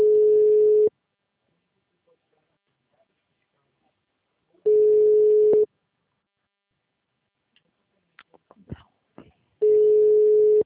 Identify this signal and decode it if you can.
Ringback tone (Europe, 425 Hz).